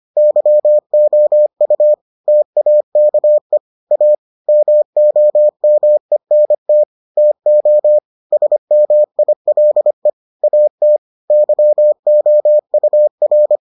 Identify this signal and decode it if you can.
Morse code, heard as a keyed tone, about 25 wpm, 600 Hz. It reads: YOU TAKE A MOMENT TO SMILE AT YOUR